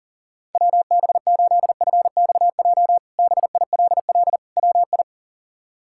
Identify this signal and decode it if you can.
Morse code, heard as a keyed tone, about 40 wpm, 700 Hz. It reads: WB8FXJ BILL WI
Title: K1USN SST QSO Practice - 40wpm